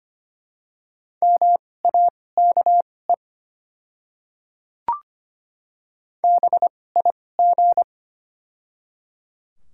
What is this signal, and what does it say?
Morse code, heard as a keyed tone, about 25 wpm, spaced as for 18 wpm, 700 Hz. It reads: MAKE BIG
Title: 2x Character Spacing - Top 500 Words - 25wpm